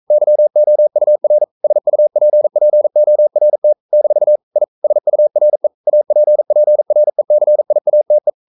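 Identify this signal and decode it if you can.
Morse code, heard as a keyed tone, about 42 wpm, 600 Hz. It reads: YOUR SUPPORT - I SURE APPRECIATE
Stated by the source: Morse code practice at 41 wpm, from CW med Gnister.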